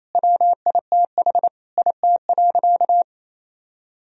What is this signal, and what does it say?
Morse code, heard as a keyed tone, about 27 wpm, 700 Hz. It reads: WITH IT.
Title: Sentences from Top 1000 Words - Review Part 3 - 28wpm